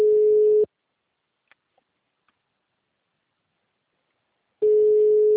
Ringback tone (Europe, 425 Hz).
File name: out- -2032- - - .6